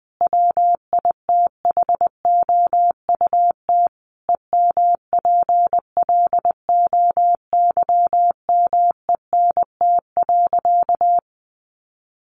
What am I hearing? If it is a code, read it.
Morse code, heard as a keyed tone, about 20 wpm, 700 Hz. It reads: WITHOUT EMPLOYMENT.
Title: Sentences from Top 1500 Words 20wpm